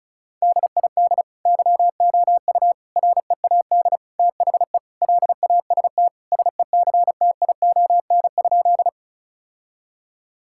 Morse code, heard as a keyed tone, about 36 wpm, 700 Hz. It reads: DID YOU READ THE LAST SECTION?
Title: Sentences from Top 1000 Words - Review Part 5 - 35wpm